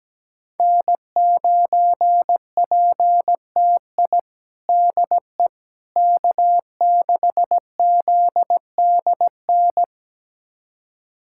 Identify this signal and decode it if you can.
Morse code, heard as a keyed tone, about 17 wpm, 700 Hz. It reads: N9PTI DE K6ZDN
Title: USA Callsign DE Callsign - 5 Characters - 17wpm